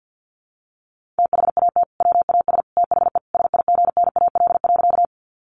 Morse code, heard as a keyed tone, about 48 wpm, 700 Hz. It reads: THAT WAS THE SIGNAL.
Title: Sentences from Top 1400 Words 50wpm